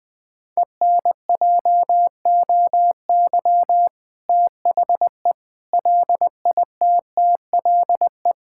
Morse code, heard as a keyed tone, about 21 wpm, 700 Hz. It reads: ENJOY THE LITTLE
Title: Difficult Sentences with Hints - 20wpm